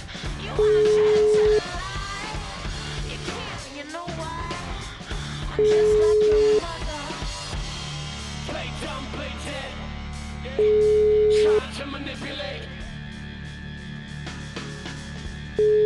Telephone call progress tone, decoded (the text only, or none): ringback tone